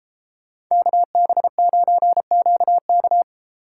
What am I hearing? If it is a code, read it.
Morse code, heard as a keyed tone, about 33 wpm, 700 Hz. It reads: KB9QK
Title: Speed Racing - USA Call Signs 28wpm